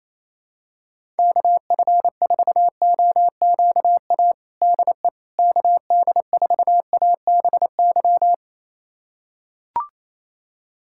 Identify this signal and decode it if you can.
Morse code, heard as a keyed tone, about 28 wpm, 700 Hz. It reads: KF4OQA DE KD4ABY
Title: USA Callsign DE Callsign 28wpm